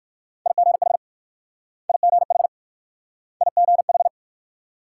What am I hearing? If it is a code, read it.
Morse code, heard as a keyed tone, about 56 wpm, 700 Hz. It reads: IGH IGH IGH
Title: Speed-Racing Multi-letter Phonemes - 45wpm